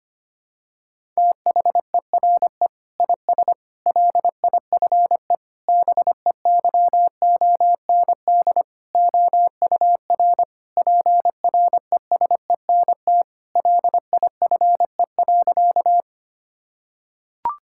Morse code, heard as a keyed tone, about 25 wpm, 700 Hz. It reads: THERE IS LIFE BEYOND OUR PRESENT LIFE.
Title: Sentences from Top 1000 Words - Review Part 6 - 25wpm